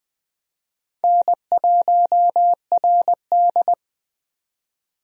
Morse code, heard as a keyed tone, about 20 wpm, 700 Hz. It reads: N1RD
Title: USA Call Signs - 4 Characters - 20wpm